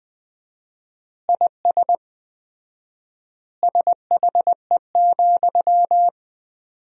Morse code, heard as a keyed tone, about 20 wpm, 700 Hz. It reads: IS SHE,
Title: 4x Word-Spacing - Sentences from Top 200 Words - 20wpm